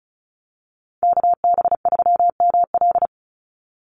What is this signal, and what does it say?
Morse code, heard as a keyed tone, about 34 wpm, 700 Hz. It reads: KB3ML
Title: Call Signs - 35wpm